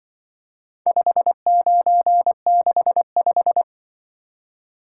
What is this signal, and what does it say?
Morse code, heard as a keyed tone, about 24 wpm, 700 Hz. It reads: 5965
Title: Speed-Racing - Four Letter-Number Combinations - 20wpm